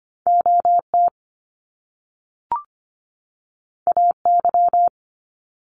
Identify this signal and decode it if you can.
Morse code, heard as a keyed tone, about 25 wpm, 700 Hz. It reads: OT AY
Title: Top Two Letter Combinations in Words 25wpm